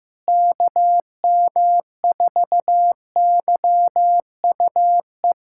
Morse code, heard as a keyed tone, about 15 wpm, 700 Hz. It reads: KM4YUE
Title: USA Callsign DE Callsign - 6 Characters - 15wpm